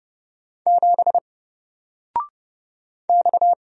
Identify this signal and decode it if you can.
Morse code, heard as a keyed tone, about 30 wpm, 700 Hz. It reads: 7 X
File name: Single-Letter-Number-30wpm